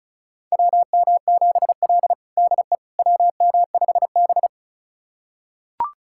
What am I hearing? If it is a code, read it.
Morse code, heard as a keyed tone, about 36 wpm, 700 Hz. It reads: WM7L DE WM5B
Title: USA Callsign DE Callsign - 4 Characters - 35wpm